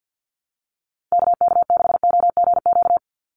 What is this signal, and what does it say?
Morse code, heard as a keyed tone, about 50 wpm, 700 Hz. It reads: KK6OGQ
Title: USA Call Signs 50wpm